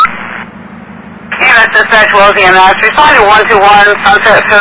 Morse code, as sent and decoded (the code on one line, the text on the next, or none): none